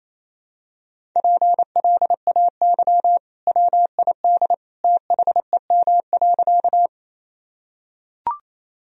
Morse code, heard as a keyed tone, about 28 wpm, 700 Hz. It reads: PLAY WID THEM.
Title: Sentences from Top 200 Words and Abbreviations 28wpm